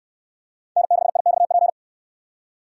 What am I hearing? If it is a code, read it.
Morse code, heard as a keyed tone, about 69 wpm, 700 Hz. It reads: N6IXP